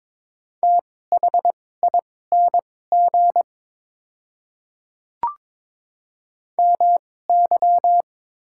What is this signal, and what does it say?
Morse code, heard as a keyed tone, about 22 wpm, spaced as for 16 wpm, 700 Hz. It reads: THING MY
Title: 2x Character Spacing - Top 100 Words - 22wpm